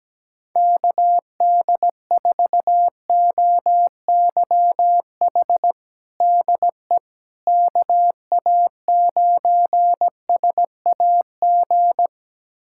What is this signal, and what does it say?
Morse code, heard as a keyed tone, about 17 wpm, 700 Hz. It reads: KD4OYH DE KA9SAG
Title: USA Callsign DE Callsign - 6 Characters - 17wpm